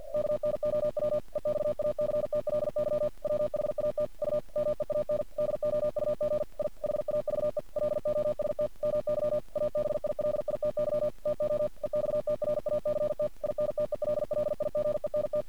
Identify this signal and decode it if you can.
Morse code, heard as a keyed tone, about 50 wpm, 600 Hz. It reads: KNOW EXACTLY WHAT U MEAN DOUG I HAVE LOST MY ABILITY TO EXTRACT INTELLIGENT
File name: archived / CRSnet